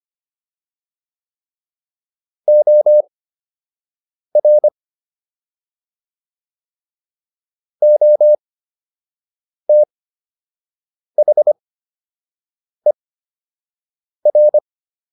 Morse code, heard as a keyed tone, about 25 wpm, spaced as for 6 wpm, 600 Hz. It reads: OR OTHER